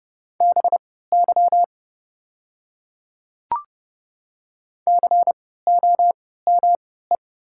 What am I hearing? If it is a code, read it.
Morse code, heard as a keyed tone, about 31 wpm, spaced as for 17 wpm, 700 Hz. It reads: BY COME
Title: 3x Character Spacing - Top 200 Words - 30wpm